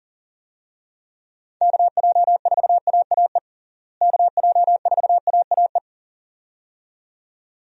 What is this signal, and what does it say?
Morse code, heard as a keyed tone, about 40 wpm, 700 Hz. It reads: KJ4AAE KJ4AAE
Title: USA Call Signs - 6 Characters - Repeated 2x - 40wpm